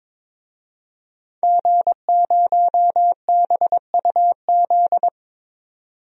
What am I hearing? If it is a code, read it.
Morse code, heard as a keyed tone, about 22 wpm, 700 Hz. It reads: G0BUZ
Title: DXCC-100 Call Signs - 5 Characters - 22wpm